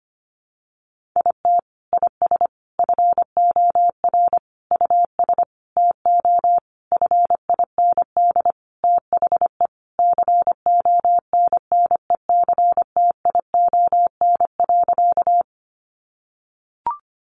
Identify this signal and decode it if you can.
Morse code, heard as a keyed tone, about 25 wpm, 700 Hz. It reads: IT IS FOR US TO FIND THE CONNECTION.
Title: Sentences from Top 1300 Words 25wpm